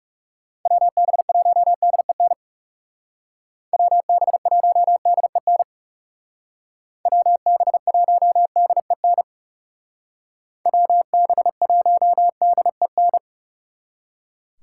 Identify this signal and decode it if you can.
Morse code, heard as a keyed tone, about 38 wpm, 700 Hz. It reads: WB1DEN WB1DEN WB1DEN WB1DEN
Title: Speed Racing - USA Call Signs 30wpm